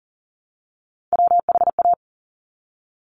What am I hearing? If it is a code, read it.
Morse code, heard as a keyed tone, about 40 wpm, 700 Hz. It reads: WHA
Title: Top Three Letter Combinations in Words 40wpm